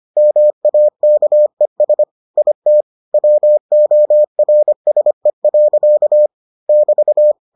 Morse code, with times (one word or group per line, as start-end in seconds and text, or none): 0.2-2.0 MAKES
2.4-2.8 IT
3.1-6.3 WORSE.
6.7-7.3 <BT>